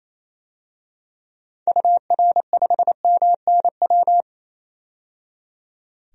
Morse code, heard as a keyed tone, about 27 wpm, 700 Hz. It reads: UR5MNW
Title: DXCC-100 Call Signs - 6 Characters - 28wpm